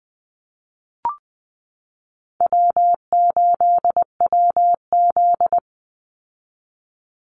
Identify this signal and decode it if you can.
Morse code, heard as a keyed tone, about 20 wpm, 700 Hz. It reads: W8WZ
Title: Easy Call Signs Repeated 3x - 20wpm